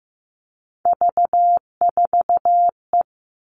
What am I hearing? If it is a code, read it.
Morse code, heard as a keyed tone, about 15 wpm, 700 Hz. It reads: V4E
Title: Three Challenging Letter-Number Combinations 15wpm